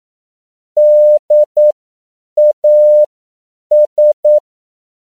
Morse code, heard as a keyed tone, about 9 wpm, spaced as for 7 wpm, 600 Hz. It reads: DAS
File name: CW / 10PPM